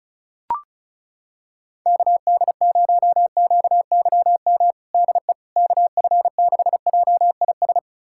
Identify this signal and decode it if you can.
Morse code, heard as a keyed tone, about 36 wpm, 700 Hz. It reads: KD0QYM DE KF6JIS
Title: USA Callsign DE Callsign - 6 Characters - 35wpm